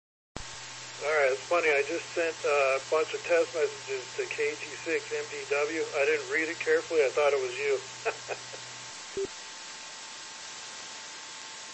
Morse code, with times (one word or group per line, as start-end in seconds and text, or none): none